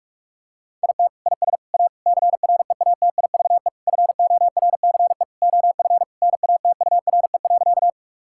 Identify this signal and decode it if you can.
Morse code, heard as a keyed tone, about 47 wpm, 700 Hz. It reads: IT IS A CREATIVE FORCE OF NATURE.
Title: Sentences from Top 1000 Words - Review Part 5 - 45wpm